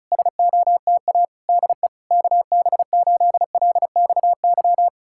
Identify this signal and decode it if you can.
Morse code, heard as a keyed tone, about 40 wpm, 700 Hz. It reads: SOTA DE KB8LXY